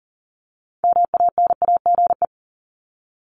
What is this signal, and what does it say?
Morse code, heard as a keyed tone, about 40 wpm, 700 Hz. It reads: MANAGE
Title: Top 1251-1500 Words 40wpm